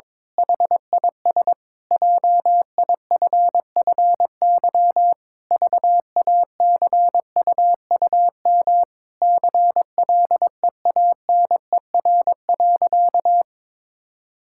Morse code, heard as a keyed tone, about 22 wpm, 700 Hz. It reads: HIS JIFFY VACUUM CLEANER.
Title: Pangram Challenge v2 - 22wpm